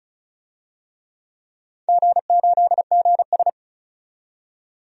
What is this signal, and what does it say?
Morse code, heard as a keyed tone, about 36 wpm, 700 Hz. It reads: G8GS
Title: DXCC-100 Call Signs - 4 Characters - 35wpm